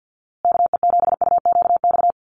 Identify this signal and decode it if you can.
Morse code, heard as a keyed tone, about 50 wpm, 700 Hz. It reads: KE7UQX
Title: USA Call Signs Repeated 2x 50wpm